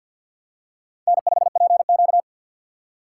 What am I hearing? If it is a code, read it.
Morse code, heard as a keyed tone, about 46 wpm, 700 Hz. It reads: N5PQ
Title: USA Call Signs - 3x Character Spacing - 50wpm